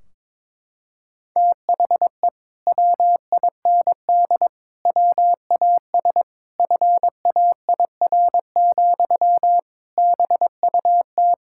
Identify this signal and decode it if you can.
Morse code, heard as a keyed tone, about 21 wpm, 700 Hz. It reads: THE WIND WAS FAIR, BUT
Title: Sentences from Top 1600 Words 22wpm